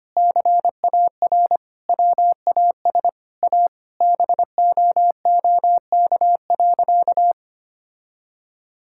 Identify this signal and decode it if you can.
Morse code, heard as a keyed tone, about 25 wpm, 700 Hz. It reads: CAR WAS A BOOK.
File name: practice / 25wpm / Sentences with 4-Word Phrases - 25wpm